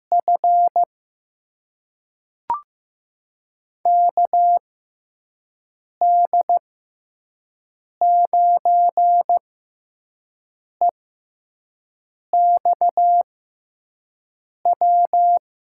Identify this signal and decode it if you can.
Morse code, heard as a keyed tone, about 15 wpm, spaced as for 5 wpm, 700 Hz. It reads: F KD9EXW
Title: USA Call Signs - 6 Characters - 6x Character Spacing - 15wpm